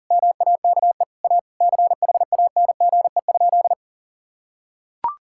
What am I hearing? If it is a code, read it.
Morse code, heard as a keyed tone, about 40 wpm, 700 Hz. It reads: MAKE A CHANGE?
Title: Sentences from Top 1600 Words 40wpm